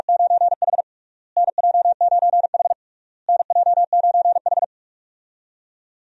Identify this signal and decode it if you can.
Morse code, heard as a keyed tone, about 46 wpm, 700 Hz. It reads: 9H NJ9H NJ9H
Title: USA Call Signs - 4 Characters - Repeated 3x - 45wpm